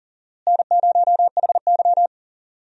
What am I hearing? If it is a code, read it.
Morse code, heard as a keyed tone, about 37 wpm, 700 Hz. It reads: N0HY
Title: USA Call Signs - 4 Characters - Repeated 3x - 40wpm